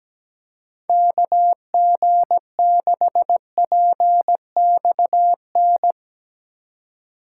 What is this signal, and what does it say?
Morse code, heard as a keyed tone, about 18 wpm, 700 Hz. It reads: KG6PXN
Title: Speed Racing - USA 2x3 Call Signs 17wpm